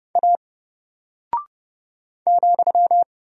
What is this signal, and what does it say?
Morse code, heard as a keyed tone, about 30 wpm, 700 Hz. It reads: A ,
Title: Single Character (Letter:Number:Punctuation:Prosign) 30wpm